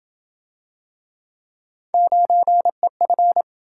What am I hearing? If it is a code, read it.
Morse code, heard as a keyed tone, about 27 wpm, 700 Hz. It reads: 9EF